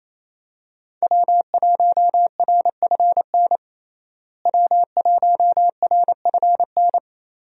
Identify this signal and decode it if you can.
Morse code, heard as a keyed tone, about 28 wpm, 700 Hz. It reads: W1RFN W1RFN
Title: USA Call Signs - 5 Characters - Repeated 3x - 28wpm